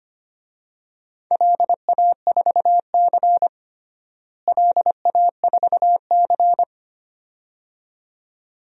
Morse code, heard as a keyed tone, about 25 wpm, 700 Hz. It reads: LA4C LA4C
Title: DXCC-100 Call Signs - 4 Characters - Repeated 2x - 25wpm